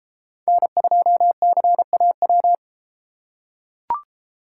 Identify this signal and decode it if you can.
Morse code, heard as a keyed tone, about 32 wpm, 700 Hz. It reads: N2CAW